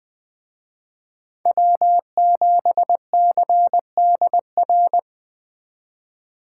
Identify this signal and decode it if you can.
Morse code, heard as a keyed tone, about 20 wpm, 700 Hz. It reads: W7CDR